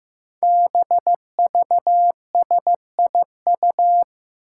Morse code, heard as a keyed tone, about 15 wpm, 700 Hz. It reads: BVSIU
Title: Five Challenging Letter-Number Combinations 15wpm